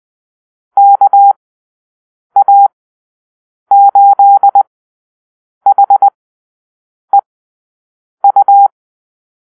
Morse code, heard as a keyed tone, about 20 wpm, spaced as for 7 wpm, 800 Hz. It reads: KA8HEU
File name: arrl / Call-Sign Drill-9